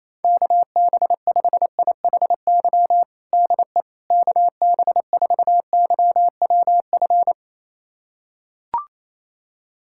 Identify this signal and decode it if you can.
Morse code, heard as a keyed tone, about 28 wpm, 700 Hz. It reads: KB5IHY DE KB4YWF
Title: USA Callsign DE Callsign - 6 Characters - 28wpm